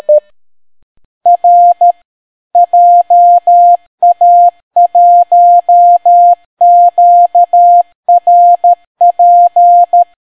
Morse code, heard as a keyed tone, about 13 wpm, 700 Hz. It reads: R JA1QRP